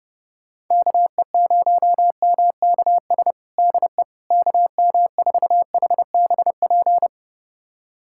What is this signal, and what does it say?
Morse code, heard as a keyed tone, about 30 wpm, 700 Hz. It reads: KE0MKS DE KM4HBP
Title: USA Callsign DE Callsign - 6 Characters - 30wpm